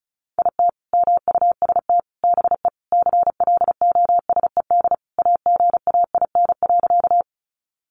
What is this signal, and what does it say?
Morse code, heard as a keyed tone, about 34 wpm, 700 Hz. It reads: IT MUST BE CLOSED AGAIN.
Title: Sentences from Top 1400 Words 35wpm